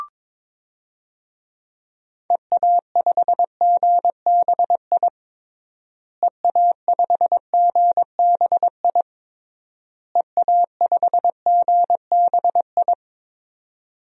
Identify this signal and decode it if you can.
Morse code, heard as a keyed tone, about 22 wpm, 700 Hz. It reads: EA5GBI EA5GBI EA5GBI
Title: DXCC-100 Call Signs - 6 Characters - Repeated 3x - 22wpm